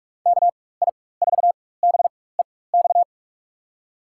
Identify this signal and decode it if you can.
Morse code, heard as a keyed tone, about 46 wpm, spaced as for 21 wpm, 700 Hz. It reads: KI4BEX
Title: USA Call Signs - 6 Characters - 4x Character Spacing - 45wpm